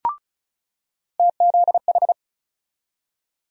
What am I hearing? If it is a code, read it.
Morse code, heard as a keyed tone, about 36 wpm, 700 Hz. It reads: TZH
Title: USA Call Sign Postfixes - 3 Letters 35wpm